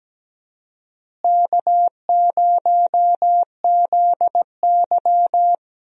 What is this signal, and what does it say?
Morse code, heard as a keyed tone, about 17 wpm, 700 Hz. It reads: K0ZY
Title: USA Call Signs - 4 Characters - Repeated 2x - 17wpm